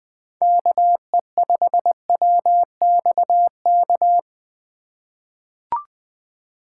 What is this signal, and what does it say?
Morse code, heard as a keyed tone, about 20 wpm, 700 Hz. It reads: KE5WXK
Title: USA Call Signs - 3x Character Spacing - 20wpm